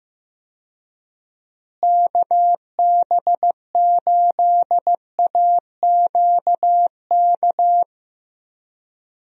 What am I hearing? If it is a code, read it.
Morse code, heard as a keyed tone, about 15 wpm, 700 Hz. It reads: KB8AQK
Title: USA Call Signs - 6 Characters - Repeated 3x - 15wpm